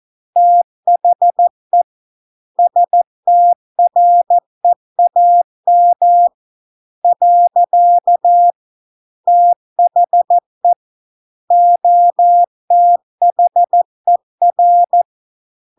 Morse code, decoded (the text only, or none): THE STREAM . THE OTHER